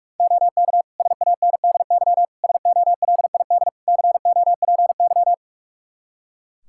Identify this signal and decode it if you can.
Morse code, heard as a keyed tone, about 45 wpm, 700 Hz. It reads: OK SANDY SOLID COPY
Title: Practice QSO - Exchange 3 - More Info - 45wpm